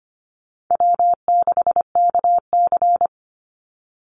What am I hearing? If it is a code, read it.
Morse code, heard as a keyed tone, about 25 wpm, 700 Hz. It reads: W6KC
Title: Call Signs - 25wpm